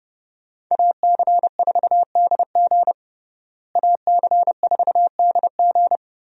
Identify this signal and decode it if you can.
Morse code, heard as a keyed tone, about 30 wpm, 700 Hz. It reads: AC4DG AC4DG
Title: USA Call Signs - 5 Characters - Repeated 3x - 30wpm